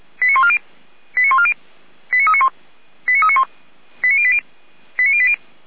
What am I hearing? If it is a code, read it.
Five-tone selective call (ZVEI), heard as quick runs of five tones, tones of about 70 ms. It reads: 89140 89140 89391 89391 89080 89080